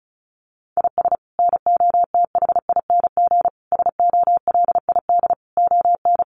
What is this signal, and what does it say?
Morse code, heard as a keyed tone, about 34 wpm, 700 Hz. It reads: IS NOTHING SOLID ON
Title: Sentences from Top 1500 Words 35wpm